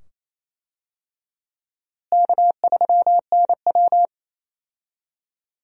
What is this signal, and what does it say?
Morse code, heard as a keyed tone, about 27 wpm, 700 Hz. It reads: K3NW
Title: USA Call Signs - 4 Characters - Repeated 2x - 28wpm